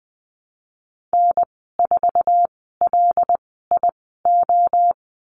Morse code, heard as a keyed tone, about 20 wpm, spaced as for 15 wpm, 700 Hz. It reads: N4LIO
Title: 076 Lesson 13g - Letter L - Call Signs 20wpm